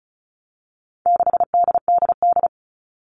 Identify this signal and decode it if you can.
Morse code, heard as a keyed tone, about 34 wpm, 700 Hz. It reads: 6DDD
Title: All Cut Numbers - 4-Digit 35wpm